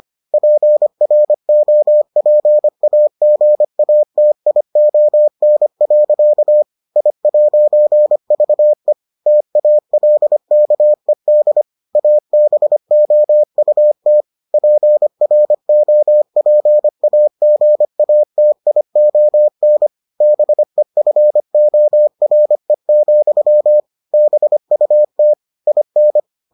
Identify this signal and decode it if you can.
Morse code, heard as a keyed tone, about 25 wpm, 600 Hz. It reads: PROPAGATION. I'VE TALKED ABOUT PROPAGATION BEFORE, BUT IN